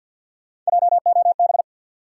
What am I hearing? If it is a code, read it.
Morse code, heard as a keyed tone, about 52 wpm, 700 Hz. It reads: JOB
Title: Sentences from Top 600 Words - New Word Drill - 50wpm